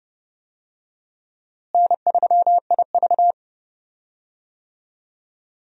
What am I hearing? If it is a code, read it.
Morse code, heard as a keyed tone, about 30 wpm, 700 Hz. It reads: N3IV
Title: USA Call Signs - 4 Characters - Repeated 2x - 30wpm